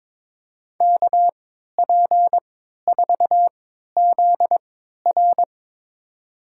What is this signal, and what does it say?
Morse code, heard as a keyed tone, about 23 wpm, spaced as for 13 wpm, 700 Hz. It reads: KP4ZR
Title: USA Call Signs - 5 Characters - 3x Character Spacing - 22wpm